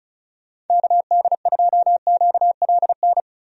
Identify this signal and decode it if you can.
Morse code, heard as a keyed tone, about 36 wpm, 700 Hz. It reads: KD2QLN
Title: USA Call Signs - 6 Characters - Repeated 2x - 35wpm